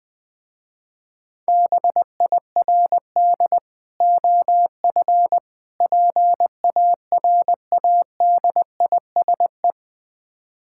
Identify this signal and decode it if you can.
Morse code, heard as a keyed tone, about 20 wpm, 700 Hz. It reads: BIRD OF PARADISE
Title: Difficult Words with Hints - 20wpm